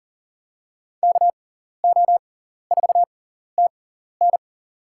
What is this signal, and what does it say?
Morse code, heard as a keyed tone, about 37 wpm, spaced as for 14 wpm, 700 Hz. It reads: KO4TN
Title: USA Call Signs - 5 Characters - 6x Character Spacing - 40wpm